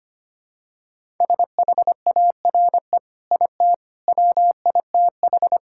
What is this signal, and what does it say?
Morse code, heard as a keyed tone, about 25 wpm, 700 Hz. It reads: SHARE IT WITH
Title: Sentences from Top 1000 Words - Review Part 6 - 25wpm